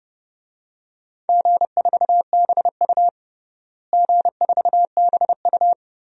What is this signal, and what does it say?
Morse code, heard as a keyed tone, about 30 wpm, 700 Hz. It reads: G4BU G4BU
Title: DXCC-100 Call Signs - 4 Characters - Repeated 3x - 30wpm